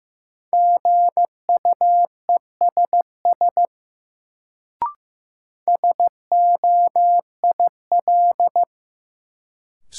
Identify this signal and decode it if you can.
Morse code, heard as a keyed tone, about 15 wpm, 700 Hz. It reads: GUESS SOIL
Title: Sentences from Top 1500 Words - New Word Drill Alt - 15wpm